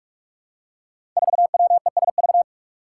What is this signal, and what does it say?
Morse code, heard as a keyed tone, about 44 wpm, 700 Hz. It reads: 4WESV
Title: Five Challenging Letter-Number Combinations 45wpm